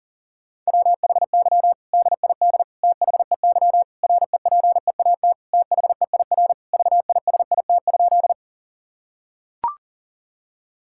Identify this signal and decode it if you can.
Morse code, heard as a keyed tone, about 37 wpm, 700 Hz. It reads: WHY DID THEY REPEAT THEIR VISIT?
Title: Sentences from Top 1600 Words 40wpm